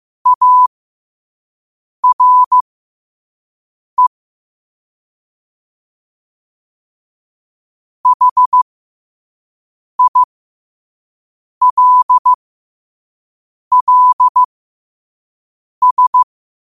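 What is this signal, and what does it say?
Morse code, heard as a keyed tone, about 15 wpm, spaced as for 5 wpm, 1000 Hz. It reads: ARE HILLS
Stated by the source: Morse code sent at 5 wpm.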